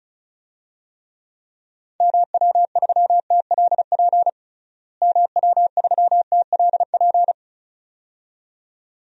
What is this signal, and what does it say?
Morse code, heard as a keyed tone, about 36 wpm, 700 Hz. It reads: MW3TLP MW3TLP
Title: DXCC-100 Call Signs - 6 Characters - Repeated 2x - 35wpm